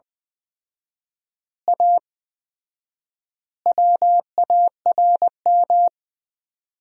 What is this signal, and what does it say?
Morse code, heard as a keyed tone, about 20 wpm, 700 Hz. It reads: A WARM
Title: 4x Word-Spacing - Sentences from Top 500 Words - 20wpm